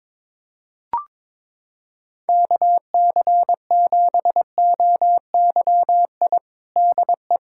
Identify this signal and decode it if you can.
Morse code, heard as a keyed tone, about 22 wpm, 700 Hz. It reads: KC7OYI DE
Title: USA Callsign DE Callsign - 6 Characters - 22wpm